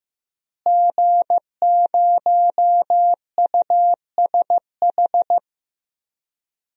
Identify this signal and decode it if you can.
Morse code, heard as a keyed tone, about 15 wpm, 700 Hz. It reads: G0USH
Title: DXCC-100 Call Signs - 5 Characters - Repeated 2x - 15wpm